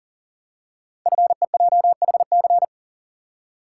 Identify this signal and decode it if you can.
Morse code, heard as a keyed tone, about 40 wpm, 690 Hz. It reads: FEJHC
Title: Code Groups - 5 Letters - 40wpm